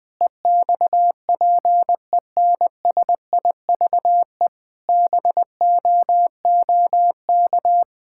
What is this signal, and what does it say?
Morse code, heard as a keyed tone, about 21 wpm, 700 Hz. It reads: EXPENSIVE BOOK
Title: Sentences from Top 1000 Words - Review Part 6 - 20wpm